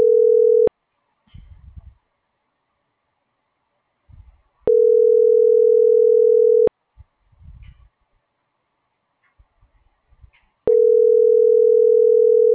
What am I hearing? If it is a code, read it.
Ringback tone (North America), 440 + 480 Hz.